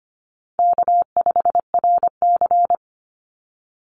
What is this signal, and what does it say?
Morse code, heard as a keyed tone, about 25 wpm, 700 Hz. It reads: K5RC
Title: Easy Call Signs - 25wpm